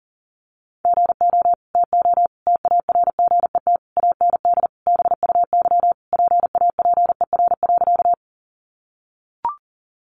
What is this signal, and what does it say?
Morse code, heard as a keyed tone, about 40 wpm, 700 Hz. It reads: GO TO TARGET AND BUY PAPER.
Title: Sentences from Top 1200 Words 40wpm